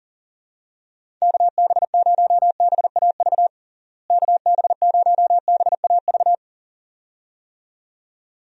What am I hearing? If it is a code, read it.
Morse code, heard as a keyed tone, about 40 wpm, 700 Hz. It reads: KB0BAV KB0BAV
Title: USA Call Signs - 6 Characters - Repeated 2x - 40wpm